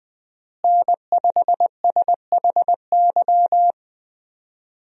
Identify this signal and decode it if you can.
Morse code, heard as a keyed tone, about 20 wpm, 700 Hz. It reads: N5SHY
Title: USA Call Signs - 5 Characters - Repeated 2x - 20wpm